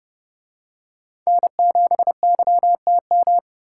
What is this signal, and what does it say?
Morse code, heard as a keyed tone, about 30 wpm, 700 Hz. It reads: N7YTM